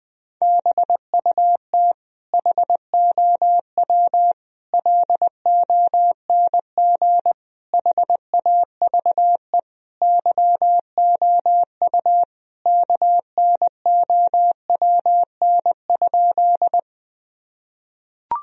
Morse code, heard as a keyed tone, about 21 wpm, 700 Hz. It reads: BUT HOW LONG HAVE YOU KNOWN?
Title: Sentences from Top 1000 Words - Review Part 3 - 20wpm